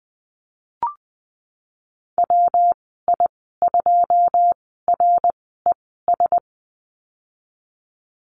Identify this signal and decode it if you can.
Morse code, heard as a keyed tone, about 19 wpm, spaced as for 14 wpm, 700 Hz. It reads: WI2RES
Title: 193 Lesson 31g - Letter v - Call Signs 20wpm